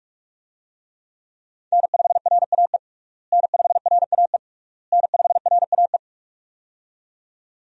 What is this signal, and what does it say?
Morse code, heard as a keyed tone, about 46 wpm, 700 Hz. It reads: N5RAE N5RAE N5RAE
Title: USA Call Signs - 5 Characters - Repeated 3x - 45wpm